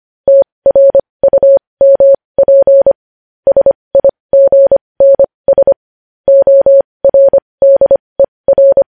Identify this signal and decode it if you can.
Morse code, heard as a keyed tone, about 25 wpm, spaced as for 20 wpm, 550 Hz. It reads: TRUMP SIGNS ORDER
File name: podcast / News headlines-1x-25x20